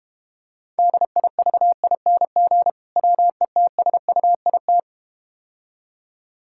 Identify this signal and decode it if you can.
Morse code, heard as a keyed tone, about 31 wpm, 700 Hz. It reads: DIVING WETSUIT
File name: practice / 25wpm / Difficult Words - Speed-Racing with Hints - 25wpm